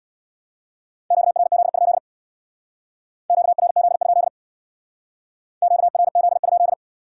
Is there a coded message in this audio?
Yes: Morse code from 1.1 to 6.7 s, heard as a keyed tone, about 66 wpm, 700 Hz. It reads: QRZ? QRZ? QRZ?